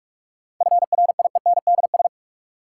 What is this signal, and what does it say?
Morse code, heard as a keyed tone, about 44 wpm, 700 Hz. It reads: FRIENDS
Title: Sentences from Top 1600 Words - New Word Drill Alt - 45wpm